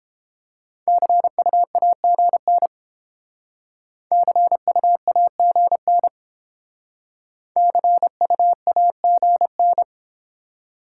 Morse code, heard as a keyed tone, about 30 wpm, 700 Hz. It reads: CUAGN CUAGN CUAGN